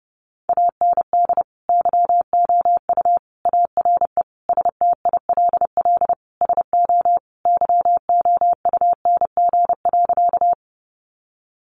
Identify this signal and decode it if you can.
Morse code, heard as a keyed tone, about 30 wpm, 700 Hz. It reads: AND YOU ARE STILL SO YOUNG.
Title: Sentences from Top 400 Words 30wpm